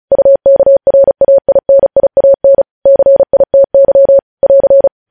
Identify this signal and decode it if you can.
Morse code, heard as a keyed tone, about 36 wpm, 550 Hz. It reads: UKRAINIAN CITY <AR>
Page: news headlines in morse code at 35 wpm